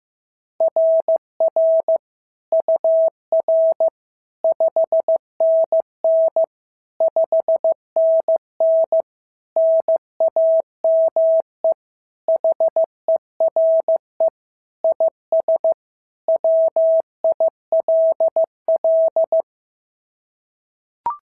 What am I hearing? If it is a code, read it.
Morse code, heard as a keyed tone, about 14 wpm, 650 Hz. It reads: RR UR 5NN 5NN NAME HERE IS WILL
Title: SOTA QSO Practice 15wpm